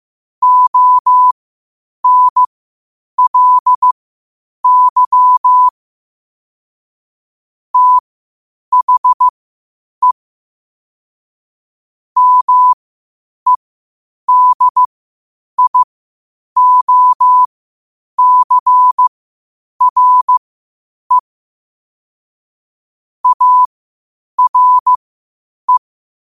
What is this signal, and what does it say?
Morse code, heard as a keyed tone, about 15 wpm, spaced as for 8 wpm, 1000 Hz. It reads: ONLY THE MEDIOCRE ARE
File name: qotdfiles / 8wpm